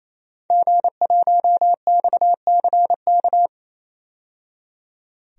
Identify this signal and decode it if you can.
Morse code, heard as a keyed tone, about 27 wpm, 700 Hz. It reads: G1XCK
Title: DXCC-100 Call Signs - 5 Characters - 28wpm